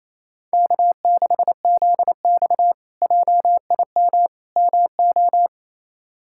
Morse code, heard as a keyed tone, about 27 wpm, 700 Hz. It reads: K6ZX JIM MO
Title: K1USN SST QSO Practice - 28wpm